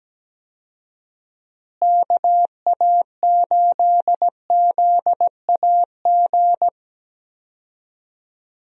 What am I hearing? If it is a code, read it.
Morse code, heard as a keyed tone, about 17 wpm, 700 Hz. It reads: KA8ZAG
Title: USA Call Signs - 6 Characters - Repeated 2x - 17wpm